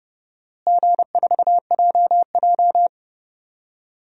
Morse code, heard as a keyed tone, about 29 wpm, 700 Hz. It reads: G4JJ
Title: DXCC-100 Call Signs - 4 Characters - Repeated 3x - 30wpm